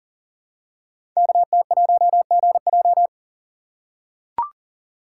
Morse code, heard as a keyed tone, about 42 wpm, 690 Hz. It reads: KT1GJ
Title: Code Groups - 5 Letter:Numbers - 40wpm